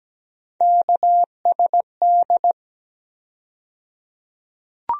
Morse code, heard as a keyed tone, about 17 wpm, 700 Hz. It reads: KSD
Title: USA Call Sign Postfixes - 3 Letters 17wpm